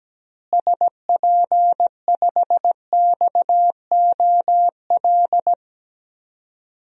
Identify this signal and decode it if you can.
Morse code, heard as a keyed tone, about 17 wpm, 700 Hz. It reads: SP5XOL
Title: DXCC-100 Call Signs - 6 Characters - Repeated 3x - 17wpm